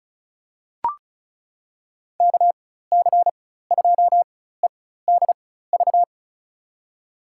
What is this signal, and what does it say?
Morse code, heard as a keyed tone, about 36 wpm, spaced as for 16 wpm, 700 Hz. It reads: KC2EDV
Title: USA Call Signs - 6 Characters - 4x Character Spacing - 35wpm